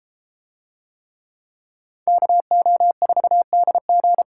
Morse code, heard as a keyed tone, about 32 wpm, 700 Hz. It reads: KO4DG